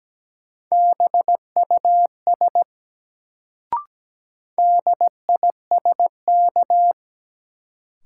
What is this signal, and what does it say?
Morse code, heard as a keyed tone, about 17 wpm, 700 Hz. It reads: BUS DISK
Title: Sentences from Top 1000 Words - Review Part 6 - New Word Drill - 17wpm